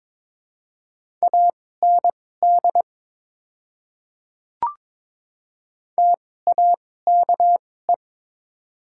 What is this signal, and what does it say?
Morse code, heard as a keyed tone, about 22 wpm, spaced as for 16 wpm, 700 Hz. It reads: AND TAKE
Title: 2x Character Spacing - Top 300 Words - 22wpm